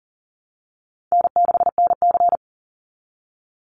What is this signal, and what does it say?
Morse code, heard as a keyed tone, about 40 wpm, 700 Hz. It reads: N6NC
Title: Call Signs Repeated 3x - 40wpm